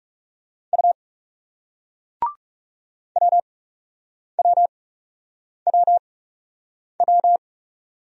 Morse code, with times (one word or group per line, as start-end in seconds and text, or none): 0.7-0.9 U
3.2-7.4 WWWW